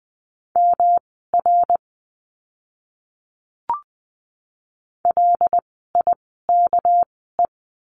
Morse code, heard as a keyed tone, about 20 wpm, spaced as for 14 wpm, 700 Hz. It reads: MR LIKE
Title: 244 Lesson 39f - Letter z - Words 20wpm